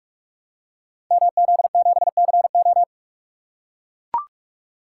Morse code, heard as a keyed tone, about 44 wpm, 700 Hz. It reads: MZ7CO
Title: Code Groups - 5 Letter:Numbers - 45wpm